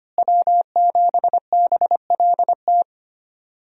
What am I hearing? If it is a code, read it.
Morse code, heard as a keyed tone, about 25 wpm, 700 Hz. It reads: W7BLT
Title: USA Call Signs - 5 Characters - Repeated 2x - 25wpm